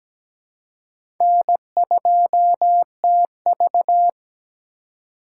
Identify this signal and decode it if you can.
Morse code, heard as a keyed tone, about 17 wpm, 700 Hz. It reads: N2TV
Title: USA Call Signs - 4 Characters - 2x Character Spacing - 17wpm